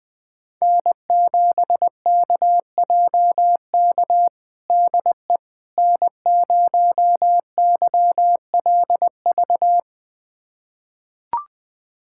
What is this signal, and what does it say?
Morse code, heard as a keyed tone, about 20 wpm, 700 Hz. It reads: N7KJK DE N0YLV
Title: USA Callsign DE Callsign - 5 Characters - 20wpm